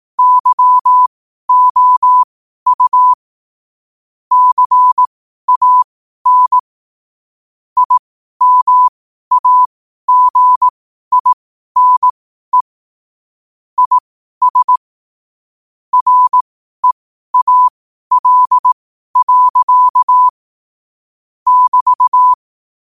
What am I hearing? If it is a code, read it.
Morse code, heard as a keyed tone, about 18 wpm, spaced as for 12 wpm, 1000 Hz. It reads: YOU CAN IMAGINE IS REAL. <BT>
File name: qotdfiles / 12wpm